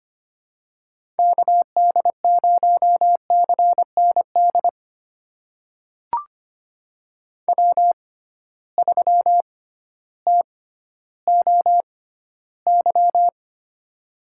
Morse code, heard as a keyed tone, about 25 wpm, spaced as for 14 wpm, 700 Hz. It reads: KD0CND W3TOY
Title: USA Call Signs - 6x Character Spacing - 25wpm